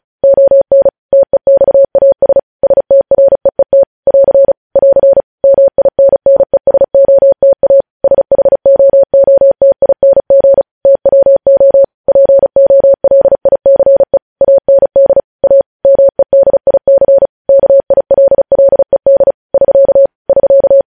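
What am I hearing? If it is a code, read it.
Morse code, heard as a keyed tone, about 36 wpm, 550 Hz. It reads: ON TEXAS STREET <AR> <AR> MINNESOTA SHOOTING TWO POLICE AND A MEDIC KILLED <SK> <SK>
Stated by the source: News headlines sent in Morse at 35 wpm.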